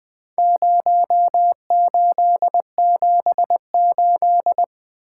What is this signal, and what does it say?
Morse code, heard as a keyed tone, about 20 wpm, 700 Hz. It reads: 0878
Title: Four Letter-Number Combinations - 20wpm